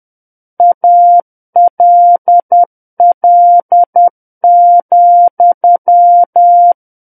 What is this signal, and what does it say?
Morse code, heard as a keyed tone, about 10 wpm, 700 Hz. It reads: ALL,
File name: Book One / 01-Chapter 1 - 10 WPM